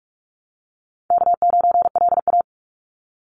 Morse code, heard as a keyed tone, about 46 wpm, 700 Hz. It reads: K9LA
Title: Call Signs - 45wpm